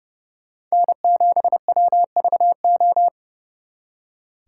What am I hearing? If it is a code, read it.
Morse code, heard as a keyed tone, about 36 wpm, 700 Hz. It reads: N7WVO
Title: Speed Racing - USA 1x3 Call Signs 30wpm